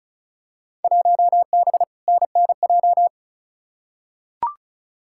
Morse code, heard as a keyed tone, about 34 wpm, 700 Hz. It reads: 1B NNJ
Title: Field Day Exchanges - 35wpm